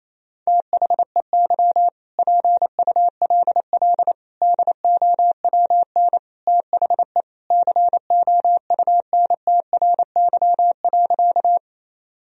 Morse code, heard as a keyed tone, about 28 wpm, 700 Hz. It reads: THEY PULL DOWN THE COUNTRY.
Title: Sentences from Top 1000 Words - Review Part 2 - 28wpm